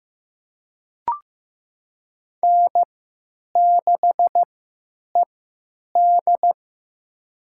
Morse code, heard as a keyed tone, about 15 wpm, spaced as for 8 wpm, 700 Hz. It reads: N6ED
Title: USA Call Signs - 4 Characters - 3x Character Spacing - 15wpm